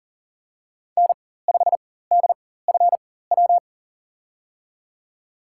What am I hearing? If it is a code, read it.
Morse code, heard as a keyed tone, about 37 wpm, spaced as for 18 wpm, 700 Hz. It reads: N5DFW
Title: USA Call Signs - 5 Characters - 4x Character Spacing - 40wpm